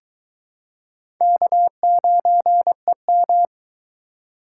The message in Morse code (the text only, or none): K9EM